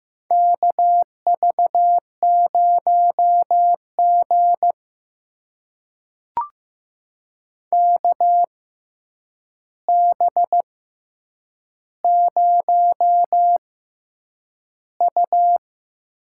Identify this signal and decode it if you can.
Morse code, heard as a keyed tone, about 15 wpm, spaced as for 8 wpm, 700 Hz. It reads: KV0G KB0U
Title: USA Call Signs - 4 Characters - 6x Character Spacing - 15wpm